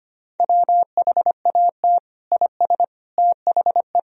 Morse code, heard as a keyed tone, about 25 wpm, 700 Hz. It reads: WHAT IS THE
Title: Sentences from Top 1000 Words - Review Part 4 - 25wpm